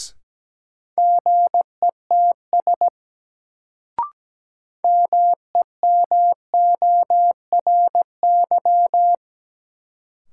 Morse code, heard as a keyed tone, about 17 wpm, 700 Hz. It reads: GETS MEMORY
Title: Sentences from Top 1300 Words - New Word Drill Alt - 17wpm